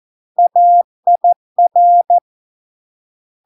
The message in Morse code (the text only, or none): AIR